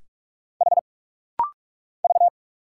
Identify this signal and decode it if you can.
Morse code, heard as a keyed tone, about 44 wpm, 700 Hz. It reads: H V
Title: Single Letters - Rapid-Fire 45wpm